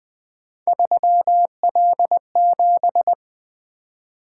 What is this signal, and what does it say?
Morse code, heard as a keyed tone, about 20 wpm, 700 Hz. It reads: 3L7
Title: Three Letter-Number Combinations - 20wpm